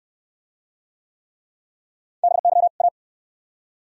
Morse code, heard as a keyed tone, about 68 wpm, 700 Hz. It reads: BY A